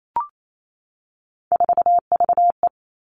Morse code, heard as a keyed tone, about 28 wpm, 700 Hz. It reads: 4VE
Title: Three Challenging Letter-Number Combinations 28wpm